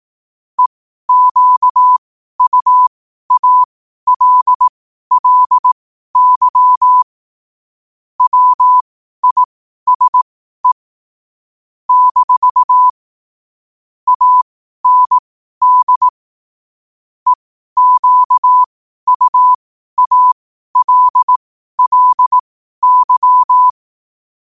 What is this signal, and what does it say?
Morse code, heard as a keyed tone, about 18 wpm, spaced as for 12 wpm, 1000 Hz. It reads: EQUALLY WISE - AND EQUALLY
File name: qotdfiles / 12wpm